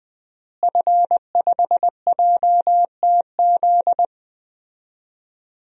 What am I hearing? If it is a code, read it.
Morse code, heard as a keyed tone, about 20 wpm, 700 Hz. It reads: F5JTZ
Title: DXCC-100 Call Signs - 5 Characters - Repeated 2x - 20wpm